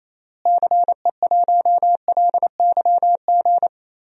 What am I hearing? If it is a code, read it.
Morse code, heard as a keyed tone, about 28 wpm, 700 Hz. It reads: CE1LYG
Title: DXCC-100 Call Signs - 6 Characters - Repeated 3x - 28wpm